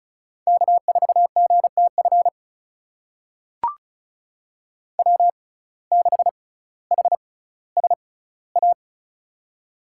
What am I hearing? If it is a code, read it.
Morse code, heard as a keyed tone, about 36 wpm, spaced as for 18 wpm, 700 Hz. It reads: K4GTF W6HSA
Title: USA Call Signs - 5 Characters - 6x Character Spacing - 35wpm